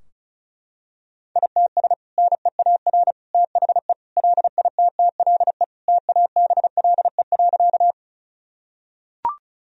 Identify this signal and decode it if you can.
Morse code, heard as a keyed tone, about 36 wpm, 700 Hz. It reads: ITS NEAR THE LITTLE TABLE.
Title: Sentences from Top 1000 Words - Review Part 2 - 35wpm